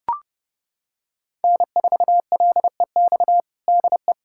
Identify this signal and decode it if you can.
Morse code, heard as a keyed tone, about 30 wpm, 700 Hz. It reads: N4LEX DE
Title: USA Callsign DE Callsign - 5 Characters - 30wpm